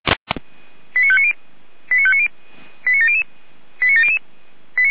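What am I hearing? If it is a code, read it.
Five-tone selective call (ZVEI), heard as quick runs of five tones, tones of about 70 ms. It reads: 89590 89590 89700 89700
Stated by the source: ZVEI five-tone selective call received by SDR, address 89020.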